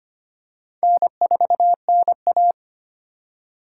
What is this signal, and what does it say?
Morse code, heard as a keyed tone, about 25 wpm, 700 Hz. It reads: N4NA
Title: USA Call Signs - 3x Character Spacing - 25wpm